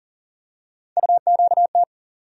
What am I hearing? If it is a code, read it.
Morse code, heard as a keyed tone, about 40 wpm, 700 Hz. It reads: UQT
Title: USA Call Sign Postfixes - 3 Letters 40wpm